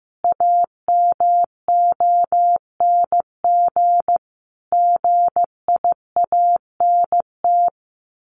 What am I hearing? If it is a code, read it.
Morse code, heard as a keyed tone, about 15 wpm, 700 Hz. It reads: AMONG GIANT
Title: A Sound Of Thunder 15wpm